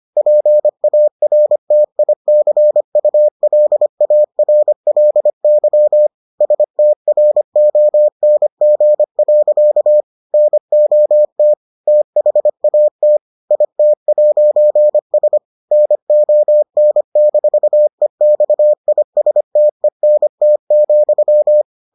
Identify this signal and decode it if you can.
Morse code, heard as a keyed tone, about 25 wpm, 600 Hz. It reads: PARTICULARLY STRONG. NOT THAT IT'S NON-EXISTENT,